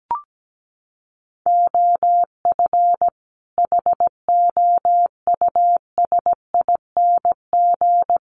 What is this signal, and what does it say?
Morse code, heard as a keyed tone, about 17 wpm, 700 Hz. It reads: OF HOUSING
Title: Sets of 5 Words Encore2 17wpm